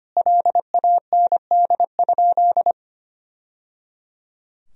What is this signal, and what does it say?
Morse code, heard as a keyed tone, about 25 wpm, 700 Hz. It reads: LAND?
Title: Sentences from Top 1000 Words - Review Part 1 - 25wpm